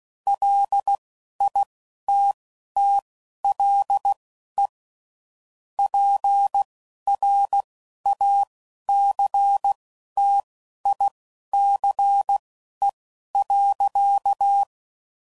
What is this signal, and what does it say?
Morse code, heard as a keyed tone, about 16 wpm, spaced as for 11 wpm, 800 Hz. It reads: LITTLE PRACTICE.
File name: CW Code Groups @10WPM / QSOs / QSO WD0FBT